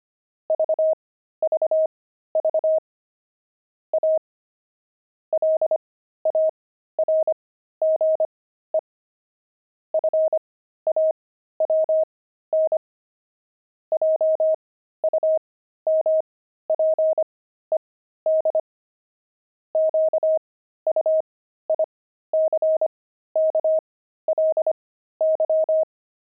Morse code, heard as a keyed tone, about 25 wpm, spaced as for 13 wpm, 630 Hz. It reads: VVV A LARGE FAWN JUMPED QUICKLY
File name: pangram-fs-13-cs-25-zinc